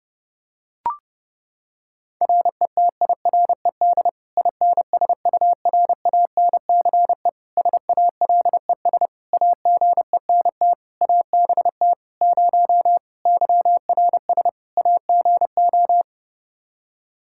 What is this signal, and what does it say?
Morse code, heard as a keyed tone, about 30 wpm, 700 Hz. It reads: RETIRED INSURANCE SALES AGENT ABT 0 YRS AGO
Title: Practice QSO - Exchange 3 - More Info - 30wpm